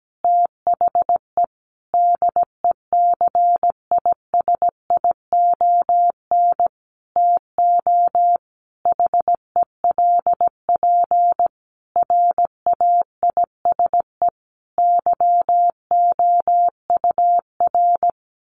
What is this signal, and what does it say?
Morse code, heard as a keyed tone, about 17 wpm, 700 Hz. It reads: THE DECISION TO HELP RAISE YOUR
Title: Sentences from Top 1400 Words 17wpm